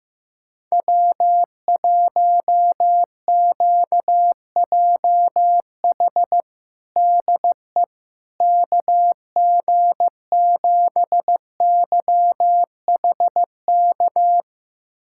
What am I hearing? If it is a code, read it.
Morse code, heard as a keyed tone, about 15 wpm, 700 Hz. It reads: W1QJH DE KG7YHK
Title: USA Callsign DE Callsign 15wpm